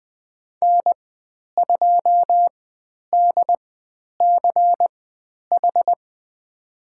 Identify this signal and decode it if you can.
Morse code, heard as a keyed tone, about 20 wpm, spaced as for 10 wpm, 700 Hz. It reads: N2DCH
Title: 095 Lesson 16g - Letter c - Call Signs 20wpm